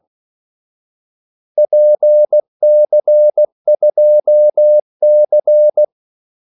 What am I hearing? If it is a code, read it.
Morse code, heard as a keyed tone, about 16 wpm, 600 Hz. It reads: PC2C
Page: Kaldesignaler 16 wpm